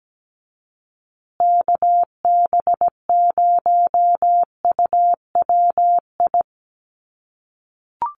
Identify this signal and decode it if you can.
Morse code, heard as a keyed tone, about 17 wpm, 700 Hz. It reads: KB0UWI
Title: USA Call Signs Repeated 2x 17wpm